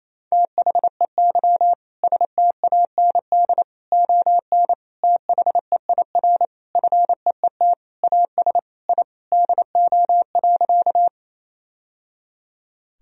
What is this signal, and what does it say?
Morse code, heard as a keyed tone, about 27 wpm, 700 Hz. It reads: THEY STAND ON THEIR FEET AS I DO.
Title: Sentences from Top 1000 Words - Review Part 1 - 28wpm